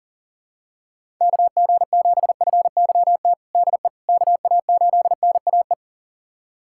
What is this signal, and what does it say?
Morse code, heard as a keyed tone, about 40 wpm, 700 Hz. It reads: KG7FYT DE KA8NAE
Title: USA Callsign DE Callsign - 6 Characters - 40wpm